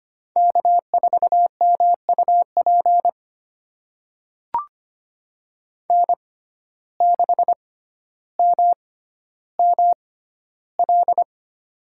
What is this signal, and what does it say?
Morse code, heard as a keyed tone, about 25 wpm, spaced as for 13 wpm, 700 Hz. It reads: K4MUP N6MML
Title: USA Call Signs - 5 Characters - 6x Character Spacing - 25wpm